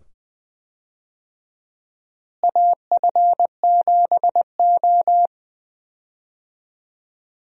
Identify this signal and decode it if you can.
Morse code, heard as a keyed tone, about 20 wpm, 700 Hz. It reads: AF7O
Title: USA Call Signs - 4 Characters - Repeated 2x - 20wpm